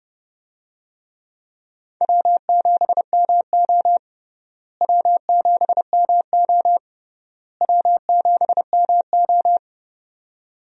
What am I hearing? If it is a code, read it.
Morse code, heard as a keyed tone, about 30 wpm, 700 Hz. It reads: W7MO W7MO W7MO
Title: USA Call Signs - 4 Characters - Repeated 3x - 30wpm